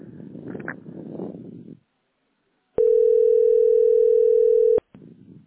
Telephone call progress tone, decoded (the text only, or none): ringback tone